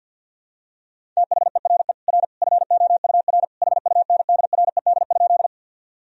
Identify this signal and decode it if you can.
Morse code, heard as a keyed tone, about 51 wpm, 700 Hz. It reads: THERE R FOUR HUNDRED?
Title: Sentences from Top 400 Words and Abbreviations 50wpm